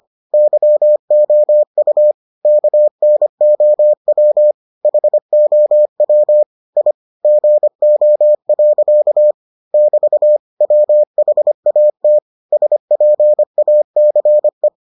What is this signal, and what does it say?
Morse code, heard as a keyed tone, about 25 wpm, 600 Hz. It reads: YOU KNOW HOW I GO. <BT> WHAT SPACE